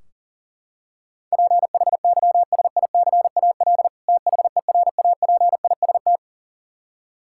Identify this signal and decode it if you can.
Morse code, heard as a keyed tone, about 40 wpm, 700 Hz. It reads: PHYSICAL THERAPIST
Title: K1USN SST QSO Practice 40wpm